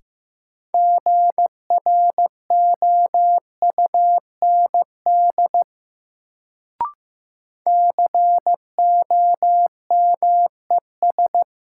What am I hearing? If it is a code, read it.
Morse code, heard as a keyed tone, about 16 wpm, 700 Hz. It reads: GROUND COMES
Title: Sets of 3 Words - New Word Drill - 15wpm